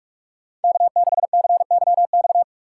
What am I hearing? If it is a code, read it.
Morse code, heard as a keyed tone, about 44 wpm, 700 Hz. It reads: K6CYX